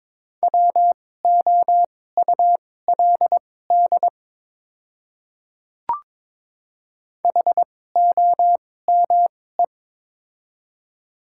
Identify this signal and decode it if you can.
Morse code, heard as a keyed tone, about 22 wpm, spaced as for 16 wpm, 700 Hz. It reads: WOULD HOME
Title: 2x Character Spacing - Top 200 Words - 22wpm